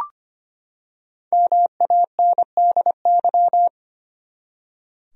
Morse code, heard as a keyed tone, about 25 wpm, 700 Hz. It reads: MANDY
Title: USA Common Names - Max 5 Letters - 25wpm